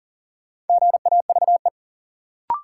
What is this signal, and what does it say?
Morse code, heard as a keyed tone, about 40 wpm, 700 Hz. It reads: GAVE
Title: Sentences from Top 500 Words - New Word Drill - 40wpm